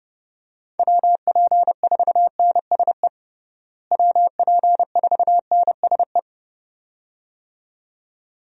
Morse code, heard as a keyed tone, about 30 wpm, 700 Hz. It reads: WP4NSE WP4NSE
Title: USA Call Signs - 6 Characters - Repeated 2x - 30wpm